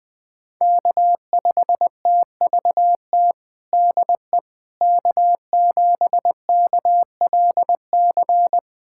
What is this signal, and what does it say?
Morse code, heard as a keyed tone, about 20 wpm, 700 Hz. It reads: K5TVT DE K7KLC
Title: USA Callsign DE Callsign - 5 Characters - 20wpm